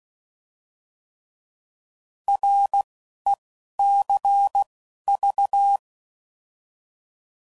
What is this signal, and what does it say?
Morse code, heard as a keyed tone, about 16 wpm, spaced as for 12 wpm, 800 Hz. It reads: RECV
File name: Mixed / Abreviations / Abreviations